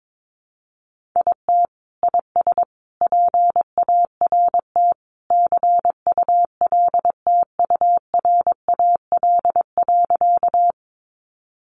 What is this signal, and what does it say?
Morse code, heard as a keyed tone, about 22 wpm, 700 Hz. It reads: IT IS PART CULTURAL.
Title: Sentences from Top 1500 Words 22wpm